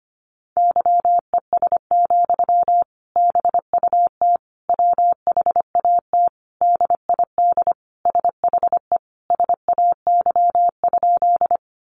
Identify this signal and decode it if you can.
Morse code, heard as a keyed tone, about 25 wpm, 700 Hz. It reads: YES, BUT WHAT DID SHE SAY?
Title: Sentences from Top 500 Words 25wpm